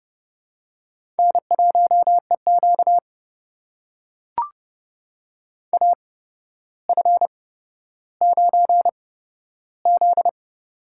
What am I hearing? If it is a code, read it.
Morse code, heard as a keyed tone, about 30 wpm, spaced as for 13 wpm, 700 Hz. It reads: N1EQ AF9Z
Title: USA Call Signs - 4 Characters - 8x Character Spacing - 30wpm